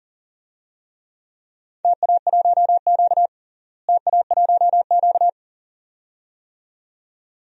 Morse code, heard as a keyed tone, about 39 wpm, 700 Hz. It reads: TA1Q TA1Q
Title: DXCC-100 Call Signs - 4 Characters - Repeated 2x - 40wpm